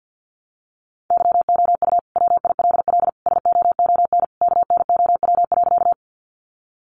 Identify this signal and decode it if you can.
Morse code, heard as a keyed tone, about 50 wpm, 700 Hz. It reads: YOU WILL SOON KNOW.
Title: Sentences from Top 400 Words 50wpm